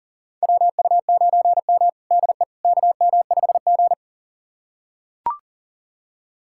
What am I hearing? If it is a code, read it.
Morse code, heard as a keyed tone, about 40 wpm, 700 Hz. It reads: WU9M DE KM5G
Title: USA Callsign DE Callsign - 4 Characters - 40wpm